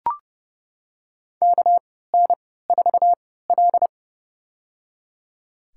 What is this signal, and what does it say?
Morse code, heard as a keyed tone, about 30 wpm, spaced as for 17 wpm, 700 Hz. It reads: KN4L
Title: USA Call Signs - 4 Characters - 3x Character Spacing - 30wpm